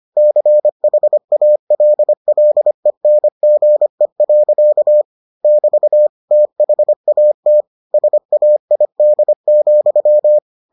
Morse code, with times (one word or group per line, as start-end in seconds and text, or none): 0.2-5.0 CHALLENGE.
5.4-6.1 <BT>
6.3-7.6 THAT
7.9-10.4 SAID,